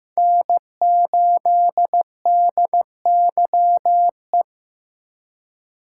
Morse code, heard as a keyed tone, about 15 wpm, 700 Hz. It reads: N8DYE
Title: USA Call Signs - 5 Characters - 4x Character Spacing - 15wpm